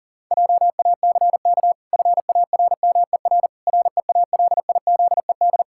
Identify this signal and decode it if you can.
Morse code, heard as a keyed tone, about 40 wpm, 700 Hz. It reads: JACK FARMER REALIZED
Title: Pangram Challenge v2 - 40wpm